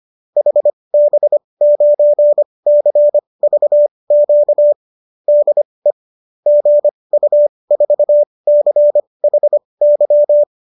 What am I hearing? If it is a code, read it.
Morse code, heard as a keyed tone, about 25 wpm, spaced as for 20 wpm, 590 Hz. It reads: HB9CVQ DE GU4CHY